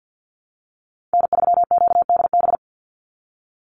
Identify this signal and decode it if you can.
Morse code, heard as a keyed tone, about 50 wpm, 700 Hz. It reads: N3QDB
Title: USA Call Signs 50wpm